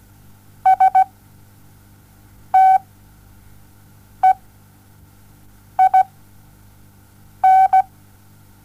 Morse code, heard as a keyed tone, about 17 wpm, spaced as for 5 wpm, 760 Hz. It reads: STEIN
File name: Onlinelehrgang / morsen / DJ4UF Morsen07